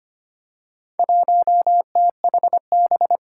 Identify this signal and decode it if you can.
Morse code, heard as a keyed tone, about 25 wpm, 700 Hz. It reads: 1THB
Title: Four Challenging Letter-Number Combinations 25wpm